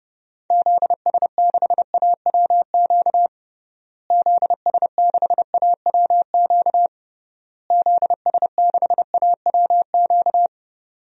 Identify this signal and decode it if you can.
Morse code, heard as a keyed tone, about 30 wpm, 700 Hz. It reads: ZS6AWQ ZS6AWQ ZS6AWQ
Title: DXCC-100 Call Signs - 6 Characters - Repeated 3x - 30wpm